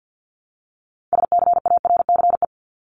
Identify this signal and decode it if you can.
Morse code, heard as a keyed tone, about 49 wpm, 700 Hz. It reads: SCARCE
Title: LY Suffix - Top 100 Words with Base Words 50wpm